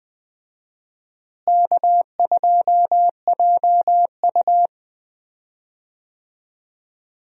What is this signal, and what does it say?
Morse code, heard as a keyed tone, about 20 wpm, 700 Hz. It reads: K2JU
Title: USA Call Signs - 4 Characters - Repeated 2x - 20wpm